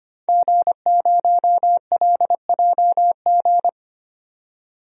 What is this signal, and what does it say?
Morse code, heard as a keyed tone, about 25 wpm, 700 Hz. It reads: G0LJG
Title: DXCC-100 Call Signs - 5 Characters - 25wpm